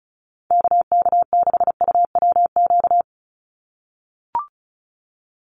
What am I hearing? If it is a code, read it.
Morse code, heard as a keyed tone, about 36 wpm, 700 Hz. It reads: KK6UWQ
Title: USA Call Signs 35wpm